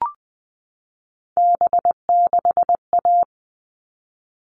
Morse code, heard as a keyed tone, about 20 wpm, 700 Hz. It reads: B6A
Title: All Cut Numbers - 3-Digit 20wpm